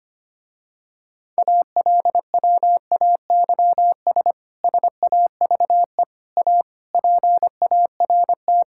Morse code, heard as a keyed tone, about 25 wpm, 700 Hz. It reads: ALWAYS SAVE A PART
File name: practice / 25wpm / Difficult Sentences with Hints - 25wpm